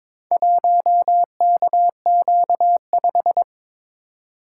Morse code, heard as a keyed tone, about 23 wpm, 700 Hz. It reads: 1KQ5
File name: practice / 22wpm / Four Letter-Number Combinations - 22wpm